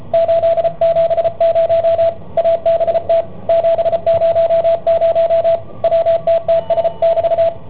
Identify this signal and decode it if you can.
Morse code, heard as a keyed tone, about 33 wpm, 670 Hz, with radio noise. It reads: 870 ABT 700 WTTS<BT>